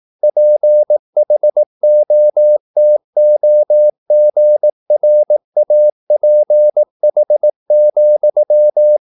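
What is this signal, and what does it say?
Morse code, heard as a keyed tone, about 18 wpm, 600 Hz. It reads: PHOTOGRAPH,
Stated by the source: Morse practice at 18 wpm, English text, from CW med Gnister.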